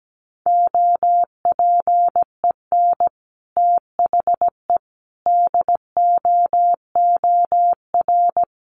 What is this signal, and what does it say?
Morse code, heard as a keyed tone, about 17 wpm, 700 Hz. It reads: OPEN THE DOOR
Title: Sentences from Top 1300 Words 17wpm